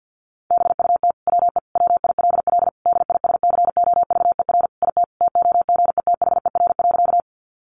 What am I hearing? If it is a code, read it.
Morse code, heard as a keyed tone, about 50 wpm, 700 Hz. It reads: BUT WE WILL DISCOVER IT TOGETHER.
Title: Sentences from Top 1100 Words 50wpm